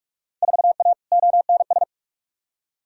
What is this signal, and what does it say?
Morse code, heard as a keyed tone, about 47 wpm, 700 Hz. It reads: 4A ONS
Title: Field Day Exchanges - 45wpm